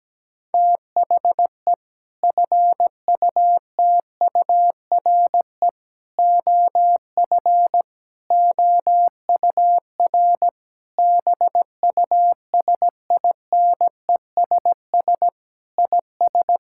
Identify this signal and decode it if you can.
Morse code, heard as a keyed tone, about 17 wpm, 700 Hz. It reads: THE FUTURE OF OUR BUSINESS IS
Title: Sentences from Top 1000 Words - Review Part 6 - 17wpm